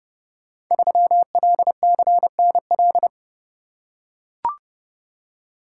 Morse code, heard as a keyed tone, about 28 wpm, 700 Hz. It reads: 3LCNL
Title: Code Groups - 5 Letter:Numbers - 30wpm